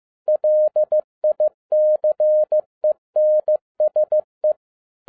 Morse code, heard as a keyed tone, about 15 wpm, 610 Hz. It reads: LICENSE